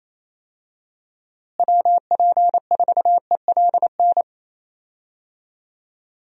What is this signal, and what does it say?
Morse code, heard as a keyed tone, about 28 wpm, 700 Hz. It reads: WP4ELN
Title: USA Call Signs - 6 Characters - Repeated 3x - 28wpm